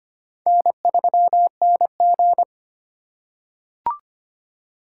Morse code, heard as a keyed tone, about 25 wpm, 700 Hz. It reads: N3NG
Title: USA Call Signs - 4 Characters - 2x Character Spacing - 25wpm